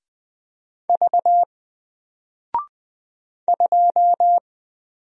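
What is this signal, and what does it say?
Morse code, heard as a keyed tone, about 20 wpm, 700 Hz. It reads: V 2
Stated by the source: Single letter and number Morse drill at 20 wpm, from N4LCD.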